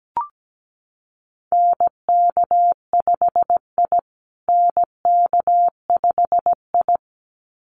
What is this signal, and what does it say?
Morse code, heard as a keyed tone, about 17 wpm, 700 Hz. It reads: NK5I NK5I
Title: USA Call Signs Repeated 2x 17wpm